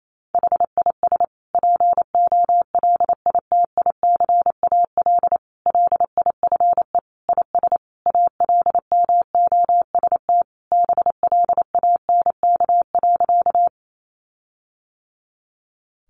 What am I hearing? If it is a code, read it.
Morse code, heard as a keyed tone, about 28 wpm, 700 Hz. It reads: HIS POLITICAL LIFE IS ALMOST BLANK.
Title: Sentences from Top 1500 Words 28wpm